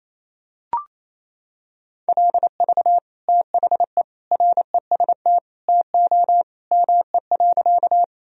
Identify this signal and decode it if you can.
Morse code, heard as a keyed tone, about 28 wpm, 700 Hz. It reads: LV THE REST TO ME.
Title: Sentences from Top 500 Words and Abbreviations 28wpm